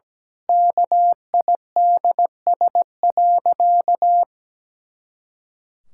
Morse code, heard as a keyed tone, about 17 wpm, 700 Hz. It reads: KIDS.
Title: Pangram Challenge v2 - 17wpm